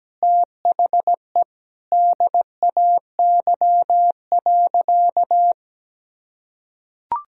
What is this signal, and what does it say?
Morse code, heard as a keyed tone, about 17 wpm, 700 Hz. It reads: THE DAY.
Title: Sentences from Top 1000 Words - Review Part 3 - 17wpm